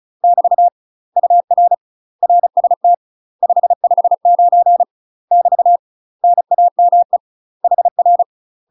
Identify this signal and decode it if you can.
Morse code, heard as a keyed tone, about 34 wpm, 700 Hz. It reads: <BT> UR RST 559 <BT> NAME HR
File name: UK MASTER QSOs TRAINING / 35WPM / UK MASTER QSOs 0005